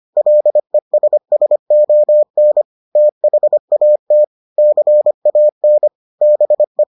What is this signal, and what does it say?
Morse code, heard as a keyed tone, about 25 wpm, 600 Hz. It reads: LESSON THAT CAN BE